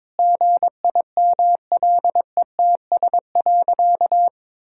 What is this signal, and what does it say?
Morse code, heard as a keyed tone, about 22 wpm, 700 Hz. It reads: GIMLETS.
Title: Pangram Challenge v2 - 22wpm